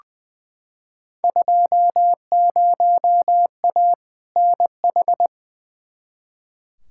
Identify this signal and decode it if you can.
Morse code, heard as a keyed tone, about 20 wpm, 700 Hz. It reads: 20A NH
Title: Field Day Exchanges Encore - 20wpm